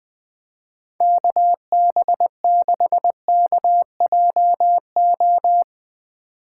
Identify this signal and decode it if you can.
Morse code, heard as a keyed tone, about 20 wpm, 700 Hz. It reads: KB6KJO